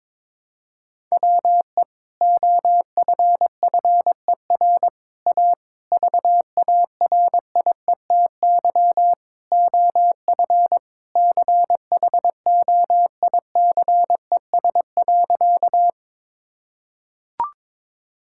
Morse code, heard as a keyed tone, about 22 wpm, 700 Hz. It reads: WE OFFER A VARIETY OF CHOICES.
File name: practice / 22wpm / Sentences with 3-Word Phrases - 22wpm